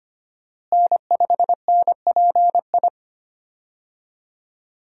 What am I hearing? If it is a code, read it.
Morse code, heard as a keyed tone, about 25 wpm, 700 Hz. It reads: N5NPI
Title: USA Call Signs - 5 Characters - Repeated 2x - 25wpm